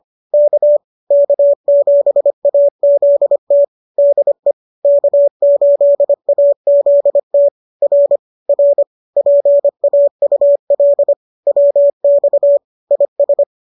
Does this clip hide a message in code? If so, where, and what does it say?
Yes: Morse code from 0.3 to 13.4 s, heard as a keyed tone, about 25 wpm, 570 Hz. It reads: K K7AZT DE K8AZT R R PAUL WX IS